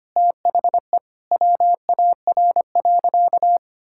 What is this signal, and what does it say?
Morse code, heard as a keyed tone, about 25 wpm, 700 Hz. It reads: THE WAR.
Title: Sentences from Top 1000 Words - Review Part 5 - 25wpm